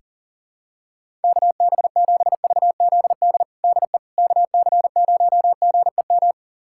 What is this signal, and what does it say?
Morse code, heard as a keyed tone, about 40 wpm, 700 Hz. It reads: KB7VZD DE KC0GEM
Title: USA Callsign DE Callsign 40wpm